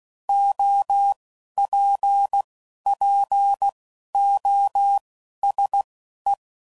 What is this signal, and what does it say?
Morse code, heard as a keyed tone, about 16 wpm, spaced as for 11 wpm, 800 Hz. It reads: OPPOSE
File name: CW Code Groups @10WPM / Technical Subjects / XL